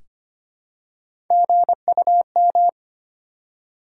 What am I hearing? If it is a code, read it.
Morse code, heard as a keyed tone, about 25 wpm, 700 Hz. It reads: GUM
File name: practice / 25wpm / Easy Words with Hints - 25wpm